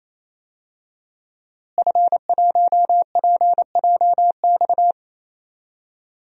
Morse code, heard as a keyed tone, about 28 wpm, 700 Hz. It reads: F1PJX
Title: DXCC-100 Call Signs - 5 Characters - Repeated 3x - 28wpm